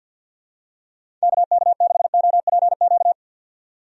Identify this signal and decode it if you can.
Morse code, heard as a keyed tone, about 51 wpm, 700 Hz. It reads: KK6OPQ
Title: USA Call Signs - 6 Characters - 50wpm